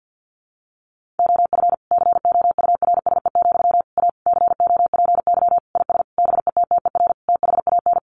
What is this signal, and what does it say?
Morse code, heard as a keyed tone, about 50 wpm, 700 Hz. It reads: OF COURSE, A COPY IS BETTER THAN
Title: Sentences from Top 1400 Words 50wpm